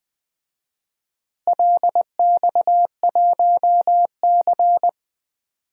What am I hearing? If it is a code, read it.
Morse code, heard as a keyed tone, about 20 wpm, 700 Hz. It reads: LX1C
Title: DXCC-100 Call Signs - 4 Characters - Repeated 2x - 20wpm